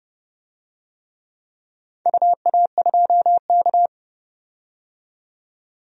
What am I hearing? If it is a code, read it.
Morse code, heard as a keyed tone, about 30 wpm, 700 Hz. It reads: UA2K
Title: DXCC-100 Call Signs - 4 Characters - 30wpm